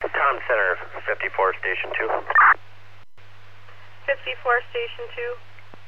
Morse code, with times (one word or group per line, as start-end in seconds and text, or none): none